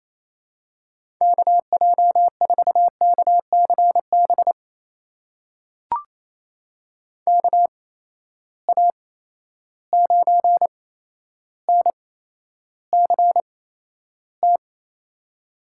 Morse code, heard as a keyed tone, about 28 wpm, spaced as for 12 wpm, 700 Hz. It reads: KJ4KCB KA9NCT
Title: USA Call Signs - 6 Characters - 8x Character Spacing - 28wpm